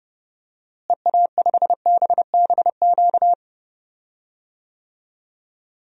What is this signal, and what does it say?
Morse code, heard as a keyed tone, about 31 wpm, 700 Hz. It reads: EA5BBQ
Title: DXCC-100 Call Signs - 6 Characters - Repeated 2x - 30wpm